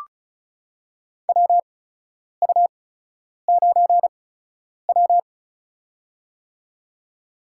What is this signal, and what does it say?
Morse code, heard as a keyed tone, about 36 wpm, spaced as for 10 wpm, 700 Hz. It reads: WU9W
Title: USA Call Signs - 8x Character Spacing - 35wpm